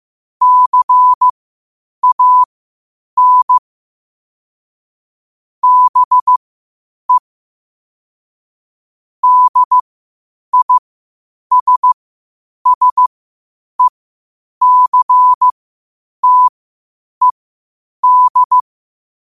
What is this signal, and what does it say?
Morse code, heard as a keyed tone, about 15 wpm, spaced as for 8 wpm, 1000 Hz. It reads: CAN BE DISSECTED
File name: qotdfiles / 8wpm